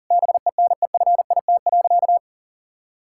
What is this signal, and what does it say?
Morse code, heard as a keyed tone, about 40 wpm, 700 Hz. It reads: BENEFIT.
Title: Sentences from Top 1000 Words - Review Part 6 - 40wpm